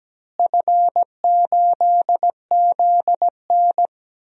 Morse code, heard as a keyed tone, about 17 wpm, 700 Hz. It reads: F8ZN
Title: DXCC-100 Call Signs - 4 Characters - Repeated 3x - 17wpm